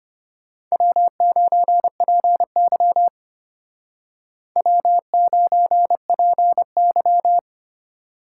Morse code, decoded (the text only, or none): W9PY W9PY